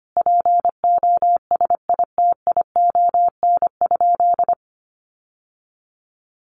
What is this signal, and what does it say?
Morse code, heard as a keyed tone, about 25 wpm, 700 Hz. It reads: POSITION?
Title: Sentences from Top 900 Words 25wpm